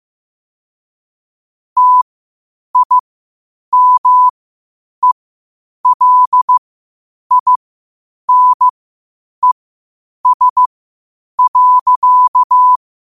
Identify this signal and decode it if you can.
Morse code, heard as a keyed tone, about 15 wpm, spaced as for 8 wpm, 1000 Hz. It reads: TIMELINES.
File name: qotdfiles / 8wpm